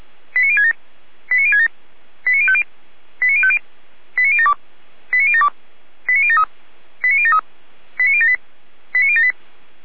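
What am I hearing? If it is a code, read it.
Five-tone selective call (ZVEI), heard as quick runs of five tones, tones of about 70 ms. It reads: 89067 89067 89050 89050 89072 89072 89073 89073 89078 89078
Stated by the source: ZVEI five-tone selective call received on SDR.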